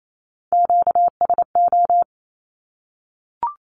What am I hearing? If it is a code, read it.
Morse code, heard as a keyed tone, about 28 wpm, 700 Hz. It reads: QSO
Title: Q-Codes Repeated 3x 28wpm